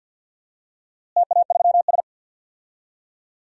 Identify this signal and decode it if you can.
Morse code, heard as a keyed tone, about 52 wpm, 700 Hz. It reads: TA3S
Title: DXCC-100 Call Signs - 4 Characters - Repeated 3x - 50wpm